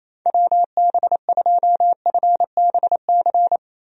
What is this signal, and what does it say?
Morse code, heard as a keyed tone, about 28 wpm, 700 Hz. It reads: WB2FBC
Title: USA Call Signs - 6 Characters - Repeated 3x - 28wpm